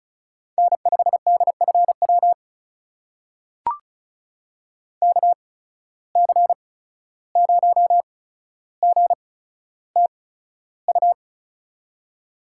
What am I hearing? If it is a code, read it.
Morse code, heard as a keyed tone, about 36 wpm, spaced as for 14 wpm, 700 Hz. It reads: N5DFW KC0GTU
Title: USA Call Signs - 8x Character Spacing - 35wpm